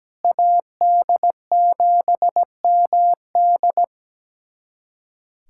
Morse code, heard as a keyed tone, about 17 wpm, 700 Hz. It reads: AD7MD
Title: USA Callsign DE Callsign - 5 Characters - 17wpm